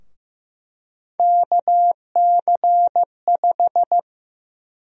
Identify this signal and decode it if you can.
Morse code, heard as a keyed tone, about 15 wpm, 700 Hz. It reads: KC5
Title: USA Call Sign Prefixes - 15wpm